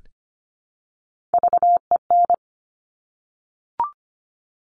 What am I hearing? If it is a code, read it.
Morse code, heard as a keyed tone, about 25 wpm, 700 Hz. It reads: VEN
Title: Top Three Letter Combinations in Words 25wpm